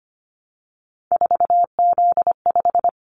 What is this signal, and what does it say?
Morse code, heard as a keyed tone, about 25 wpm, 700 Hz. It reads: 4Z5
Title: Callsign Prefixes for Top 50 DXCC - 25wpm